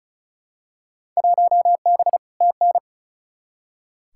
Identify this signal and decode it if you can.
Morse code, heard as a keyed tone, about 36 wpm, 700 Hz. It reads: 1B TN
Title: Field Day Exchanges - 35wpm